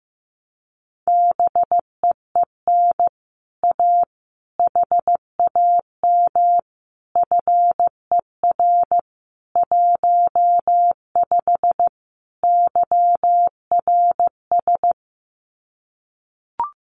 Morse code, heard as a keyed tone, about 15 wpm, 700 Hz. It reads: BEEN A HAM FER 15 YRS
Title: Common QSO Phrases Encore 15wpm